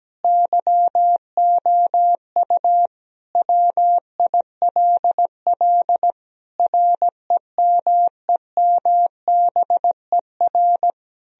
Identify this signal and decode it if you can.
Morse code, heard as a keyed tone, about 17 wpm, 700 Hz. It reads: YOU WILL REMEMBER
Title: Sentences from Top 1000 Words - Review Part 1 - 17wpm